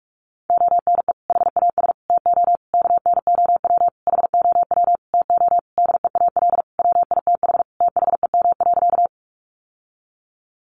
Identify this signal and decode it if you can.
Morse code, heard as a keyed tone, about 44 wpm, 700 Hz. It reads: ONE HAS TO KNOW HOW TO DEAL WITH THEM.
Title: Sentences from Top 800 Words 45wpm